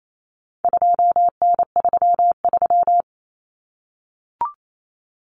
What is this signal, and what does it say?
Morse code, heard as a keyed tone, about 28 wpm, 700 Hz. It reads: 2N33
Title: Common Cut Numbers - 4-Digit 28wpm